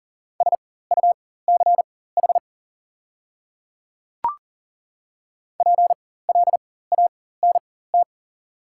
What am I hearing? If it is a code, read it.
Morse code, heard as a keyed tone, about 40 wpm, spaced as for 19 wpm, 690 Hz. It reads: SUCH PLANT
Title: 4x Character Spacing - Top 500 Words - 40wpm